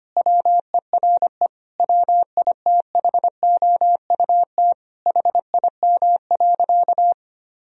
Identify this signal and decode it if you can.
Morse code, heard as a keyed tone, about 25 wpm, 700 Hz. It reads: WERE WITHOUT HIM.
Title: Sentences from Top 1000 Words - Review Part 3 - 25wpm